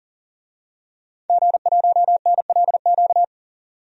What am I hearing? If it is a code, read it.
Morse code, heard as a keyed tone, about 38 wpm, 700 Hz. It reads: G1NLQ
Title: DXCC-100 Call Signs - 5 Characters - 40wpm